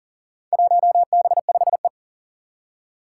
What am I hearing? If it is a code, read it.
Morse code, heard as a keyed tone, about 39 wpm, 700 Hz. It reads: 1B5E
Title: Four Challenging Letter-Number Combinations 40wpm